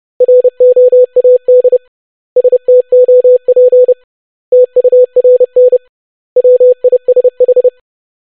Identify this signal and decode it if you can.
Morse code, heard as a keyed tone, about 30 wpm, 500 Hz. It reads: ROAD STOP TURN WISH
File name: basic500-25wpm